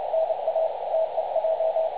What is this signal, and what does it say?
Morse code, heard as a keyed tone, about 30 wpm, 650 Hz, with radio noise. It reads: NA TEJ
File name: dlazaawansowanych / nagraniazpasma / websdr recording 3565.0kHz